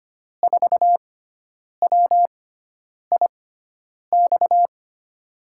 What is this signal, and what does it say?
Morse code, heard as a keyed tone, about 25 wpm, spaced as for 9 wpm, 700 Hz. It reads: 4WIX
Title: USA Call Signs - 6 Characters - 6x Character Spacing - 25wpm